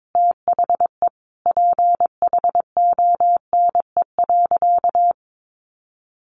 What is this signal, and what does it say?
Morse code, heard as a keyed tone, about 22 wpm, 700 Hz. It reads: THE PHONE.
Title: Sentences from Top 1100 Words 22wpm